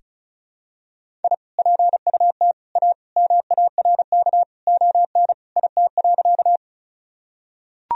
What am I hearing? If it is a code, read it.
Morse code, heard as a keyed tone, about 34 wpm, 690 Hz. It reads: I PUT A MARK ON IT.
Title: Sentences from Top 1000 Words - Review Part 1 - 35wpm